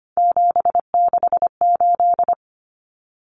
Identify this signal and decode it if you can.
Morse code, heard as a keyed tone, about 25 wpm, 700 Hz. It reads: 768
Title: Common Cut Numbers - 3-Digit 25wpm